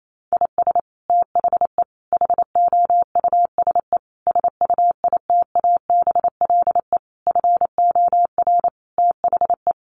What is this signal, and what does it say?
Morse code, heard as a keyed tone, about 28 wpm, 700 Hz. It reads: IS THE HOUSE SUITABLE FOR THE
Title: Sentences from Top 1400 Words 28wpm